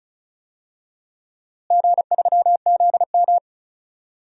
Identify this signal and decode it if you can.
Morse code, heard as a keyed tone, about 36 wpm, 700 Hz. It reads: G3ZM
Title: DXCC-100 Call Signs - 4 Characters - Repeated 3x - 35wpm